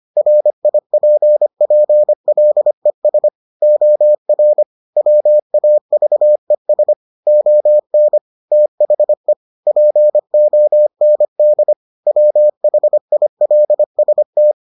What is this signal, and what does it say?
Morse code, heard as a keyed tone, about 25 wpm, 600 Hz. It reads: RIPPLES OR WAVES ON THE POND WHILST